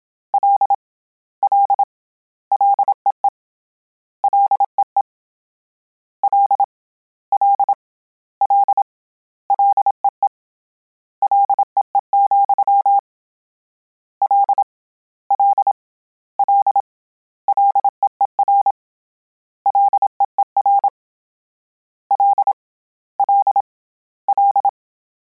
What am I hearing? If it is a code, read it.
Morse code, heard as a keyed tone, about 27 wpm, 800 Hz. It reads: L L LEE LEE L L L LEE LEE, L L L LEER LEER L L L
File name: Combinations / Combinations-mp3s / Combinations@25